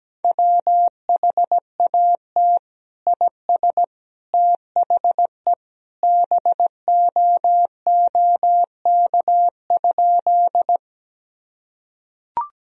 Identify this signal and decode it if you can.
Morse code, heard as a keyed tone, about 17 wpm, 700 Hz. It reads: WHAT IS THE BOOK?
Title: Sentences from Top 1000 Words - Review Part 1 - 17wpm